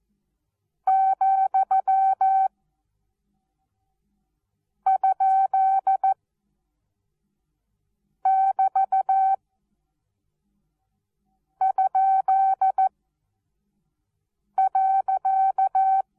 Morse code, heard as a keyed tone, about 14 wpm, spaced as for 4 wpm, 770 Hz. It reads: ,?<BT>?.
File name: Onlinelehrgang / morsen / DJ4UF Morsen19